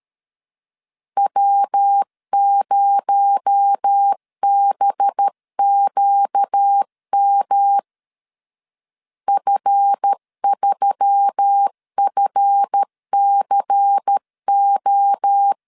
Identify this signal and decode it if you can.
Morse code, heard as a keyed tone, about 13 wpm, 800 Hz. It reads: W0BQM F3FCO
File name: LEZIONI / 7 GF9Z / 7 GF 12wpm 2122